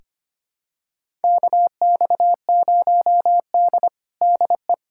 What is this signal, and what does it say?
Morse code, heard as a keyed tone, about 25 wpm, 700 Hz. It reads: KX0D DE
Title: USA Callsign DE Callsign - 4 Characters - 25wpm